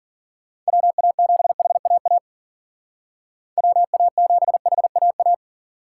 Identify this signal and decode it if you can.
Morse code, heard as a keyed tone, about 51 wpm, 700 Hz. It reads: WA7HAA WA7HAA